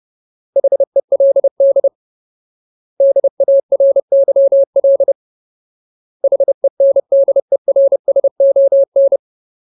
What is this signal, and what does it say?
Morse code, heard as a keyed tone, about 30 wpm, 550 Hz. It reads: HELD DARYL HENDERSON
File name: Ultimate Word List / 30WPM Ultimate